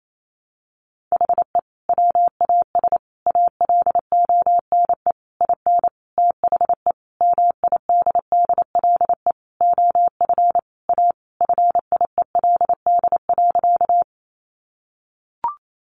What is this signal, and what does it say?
Morse code, heard as a keyed tone, about 28 wpm, 700 Hz. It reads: HE WAS ALONE IN THE MIDDLE OF A FIELD.
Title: Sentences from Top 1000 Words 28wpm